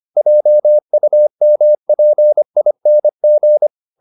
Morse code, heard as a keyed tone, about 25 wpm, 600 Hz. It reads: JUMPING
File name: the old man and the sea part 1 25wpm